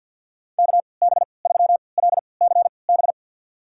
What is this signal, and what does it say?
Morse code, heard as a keyed tone, about 52 wpm, spaced as for 29 wpm, 700 Hz. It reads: KB3LCB
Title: USA Call Signs - 3x Character Spacing - 50wpm